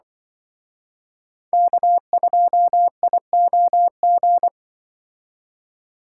Morse code, heard as a keyed tone, about 24 wpm, 700 Hz. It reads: K2IOG
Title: Speed Racing - USA 1x3 Call Signs 20wpm